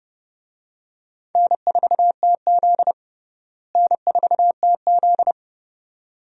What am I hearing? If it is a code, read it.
Morse code, heard as a keyed tone, about 32 wpm, 700 Hz. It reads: N4TZ N4TZ
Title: USA Call Signs - 4 Characters - Repeated 2x - 30wpm